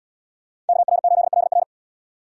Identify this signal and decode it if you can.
Morse code, heard as a keyed tone, about 77 wpm, 700 Hz. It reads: KR8LR